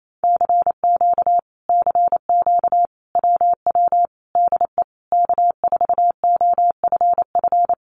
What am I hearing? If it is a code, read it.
Morse code, heard as a keyed tone, about 28 wpm, 700 Hz. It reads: CQ CQ WW DE K4OFF
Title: Common QSO Phrases Encore 28wpm